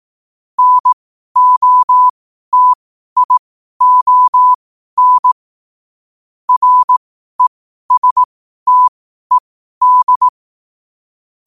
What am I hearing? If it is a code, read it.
Morse code, heard as a keyed tone, about 18 wpm, spaced as for 12 wpm, 1000 Hz. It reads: NOTION RESTED
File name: qotdfiles / 12wpm